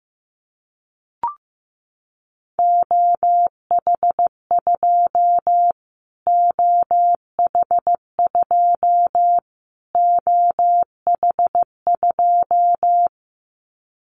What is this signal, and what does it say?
Morse code, heard as a keyed tone, about 15 wpm, 700 Hz. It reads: OH2 OH2 OH2
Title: Callsign Prefixes for Top 25 DXCC - Repeated 3x - 15wpm